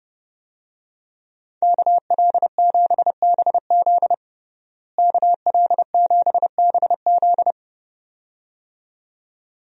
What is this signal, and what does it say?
Morse code, heard as a keyed tone, about 32 wpm, 700 Hz. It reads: KL7BZ KL7BZ
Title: DXCC-100 Call Signs - 5 Characters - Repeated 2x - 30wpm